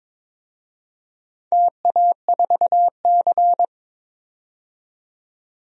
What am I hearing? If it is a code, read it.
Morse code, heard as a keyed tone, about 22 wpm, 700 Hz. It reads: TA4C
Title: DXCC-100 Call Signs - 4 Characters - 22wpm